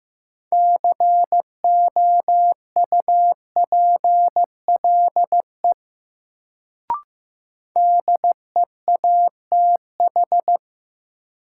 Morse code, heard as a keyed tone, about 15 wpm, 700 Hz. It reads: COUPLE DEATH
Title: Sentences from Top 1100 Words - New Word Drill - 15wpm